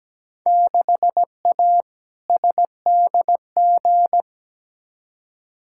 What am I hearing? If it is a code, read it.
Morse code, heard as a keyed tone, about 17 wpm, 700 Hz. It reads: 6A SDG
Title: Field Day Exchanges Encore - 17wpm